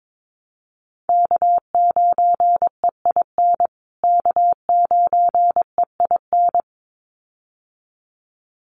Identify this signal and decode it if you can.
Morse code, heard as a keyed tone, about 22 wpm, 700 Hz. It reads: K9EIN K9EIN
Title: USA Call Signs Repeated 2x 22wpm